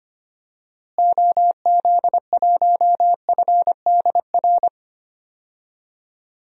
Morse code, heard as a keyed tone, about 25 wpm, 700 Hz. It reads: OZ1FDR
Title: DXCC-100 Call Signs - 6 Characters - Repeated 2x - 25wpm